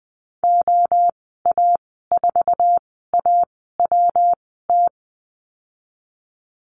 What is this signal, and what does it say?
Morse code, heard as a keyed tone, about 19 wpm, spaced as for 14 wpm, 700 Hz. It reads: OA4AWT
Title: 122 Lesson 20g - Number 3 - Call Signs 20wpm